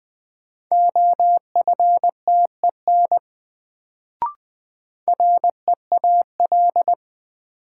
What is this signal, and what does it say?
Morse code, heard as a keyed tone, about 21 wpm, 700 Hz. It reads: OFTEN REAL
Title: Sentences from Top 300 Words - New Word Drill - 20wpm